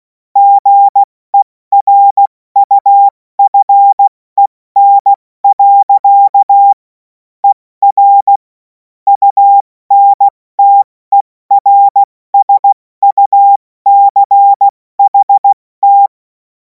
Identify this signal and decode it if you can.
Morse code, heard as a keyed tone, about 16 wpm, 800 Hz. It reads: GERUFEN. ER UNTERSUCHT
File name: Morsekurs / Witze / Witz 07. Juli 2026 70 F80